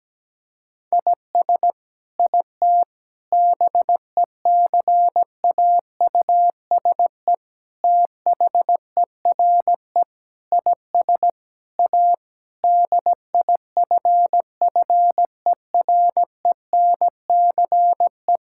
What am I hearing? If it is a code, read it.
Morse code, heard as a keyed tone, about 17 wpm, 700 Hz. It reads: IS IT BECAUSE THERE IS A DIFFERENCE
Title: Sentences from Top 1000 Words - Review Part 5 - 17wpm